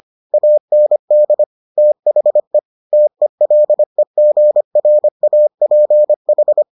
Morse code, heard as a keyed tone, about 25 wpm, 600 Hz. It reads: AND THE TELEGRAPH